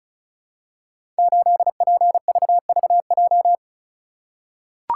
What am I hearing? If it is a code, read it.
Morse code, heard as a keyed tone, about 36 wpm, 700 Hz. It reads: 8PVVJ
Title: Code Groups - 5 Letter:Numbers - 35wpm